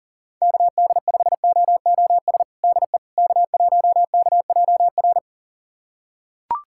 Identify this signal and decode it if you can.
Morse code, heard as a keyed tone, about 40 wpm, 700 Hz. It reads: KD5OOS DE K1KJR
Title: USA Callsign DE Callsign 40wpm